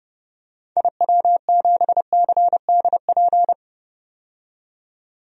Morse code, heard as a keyed tone, about 32 wpm, 700 Hz. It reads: IW7CDP
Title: DXCC-100 Call Signs - 6 Characters - 30wpm